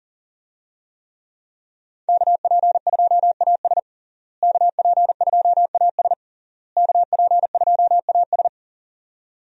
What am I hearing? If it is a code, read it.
Morse code, heard as a keyed tone, about 40 wpm, 700 Hz. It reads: KP2AS KP2AS KP2AS
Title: USA Call Signs - 5 Characters - Repeated 3x - 40wpm